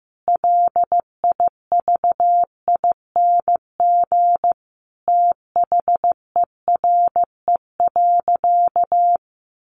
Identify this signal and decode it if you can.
Morse code, heard as a keyed tone, about 15 wpm, 700 Hz. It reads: LIVING THERE.
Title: Sentences from Top 1500 Words 15wpm